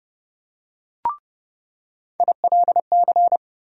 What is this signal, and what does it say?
Morse code, heard as a keyed tone, about 30 wpm, 700 Hz. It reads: ILC
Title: USA Call Sign Postfixes - 3 Letters 30wpm